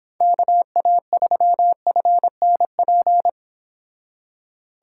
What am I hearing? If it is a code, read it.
Morse code, heard as a keyed tone, about 30 wpm, 700 Hz. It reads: KA3FNP